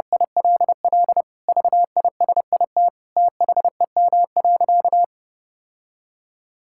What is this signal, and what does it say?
Morse code, heard as a keyed tone, about 30 wpm, 700 Hz. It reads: ILL VISIT THEM.
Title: Sentences from Top 1000 Words - Review Part 6 - 30wpm